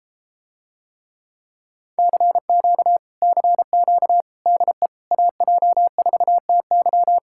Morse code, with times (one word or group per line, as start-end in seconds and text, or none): 2.0-3.0 CQ
3.2-4.2 CQ
4.5-4.9 DE
5.1-7.2 AJ4TY